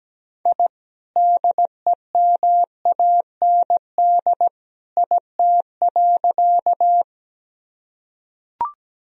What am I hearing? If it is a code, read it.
Morse code, heard as a keyed tone, about 17 wpm, 700 Hz. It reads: I DEMAND IT.
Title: Sentences from Top 1000 Words - Review Part 4 - 17wpm